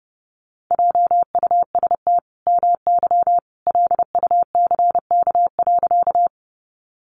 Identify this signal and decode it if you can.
Morse code, heard as a keyed tone, about 30 wpm, 700 Hz. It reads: JUST MY LUCK.
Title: Sentences from Top 1500 Words 30wpm